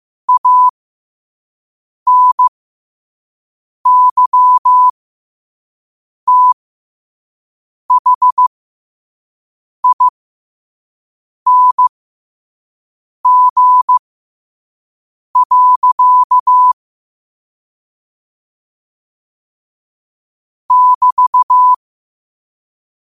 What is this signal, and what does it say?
Morse code, heard as a keyed tone, about 15 wpm, spaced as for 5 wpm, 1000 Hz. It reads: ANYTHING. <BT>